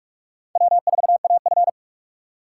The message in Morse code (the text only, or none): W4AF